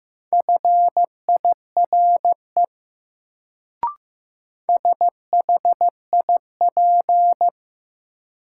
Morse code, heard as a keyed tone, about 15 wpm, 700 Hz. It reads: FIRE SHIP
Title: Sentences from Top 400 Words - New Word Drill - 15wpm